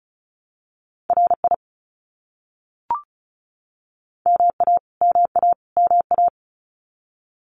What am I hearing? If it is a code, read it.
Morse code, heard as a keyed tone, about 34 wpm, 700 Hz. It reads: RI MA MA MA
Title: US States Repeated 3x 35wpm